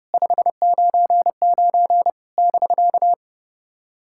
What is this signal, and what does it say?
Morse code, heard as a keyed tone, about 30 wpm, 700 Hz. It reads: 599 <BK>
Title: SOTA QSO Practice 30wpm